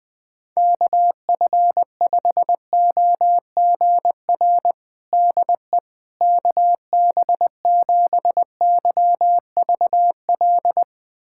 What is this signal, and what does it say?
Morse code, heard as a keyed tone, about 20 wpm, 700 Hz. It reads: KF5OGR DE KB7YVL
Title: USA Callsign DE Callsign - 6 Characters - 20wpm